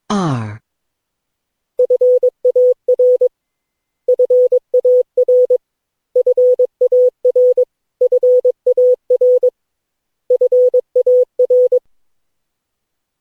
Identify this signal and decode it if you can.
Morse code, heard as a keyed tone, about 22 wpm, 500 Hz. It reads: FAR FAR FAR FAR FAR
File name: cw / Session9 cw training